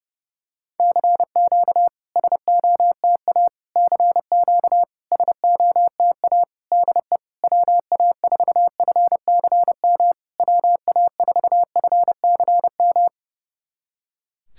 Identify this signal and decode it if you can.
Morse code, heard as a keyed tone, about 30 wpm, 700 Hz. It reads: CQ SOTA CQ SOTA DE WA4FCM WA4FCM
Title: SOTA QSO Practice 30wpm